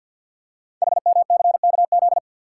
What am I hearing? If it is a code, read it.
Morse code, heard as a keyed tone, about 48 wpm, 700 Hz. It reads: HMCKZ
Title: Code Groups - 5 Letters - 50wpm